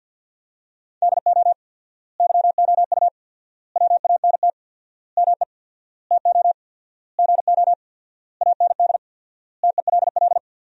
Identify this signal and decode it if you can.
Morse code, heard as a keyed tone, about 53 wpm, 700 Hz. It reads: DO YOU WANT ME TO GO AND TELL
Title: 4x Word-Spacing - Sentences from Top 200 Words - 50wpm